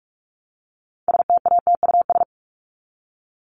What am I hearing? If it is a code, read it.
Morse code, heard as a keyed tone, about 44 wpm, 700 Hz. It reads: STATUS
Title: Top 1251-1500 Words 45wpm